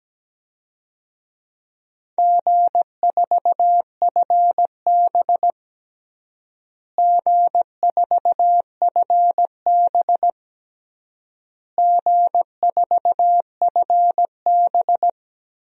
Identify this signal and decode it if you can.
Morse code, heard as a keyed tone, about 17 wpm, 700 Hz. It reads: G4FB G4FB G4FB
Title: DXCC-100 Call Signs - 4 Characters - Repeated 3x - 17wpm